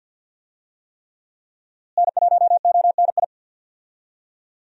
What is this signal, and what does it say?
Morse code, heard as a keyed tone, about 46 wpm, 700 Hz. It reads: N1ONI
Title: USA Call Signs - 5 Characters - Repeated 3x - 50wpm